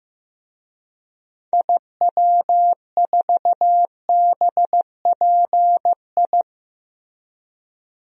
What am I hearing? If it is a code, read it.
Morse code, heard as a keyed tone, about 16 wpm, 700 Hz. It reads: IW4BPI
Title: DXCC-100 Call Signs - 6 Characters - Repeated 2x - 15wpm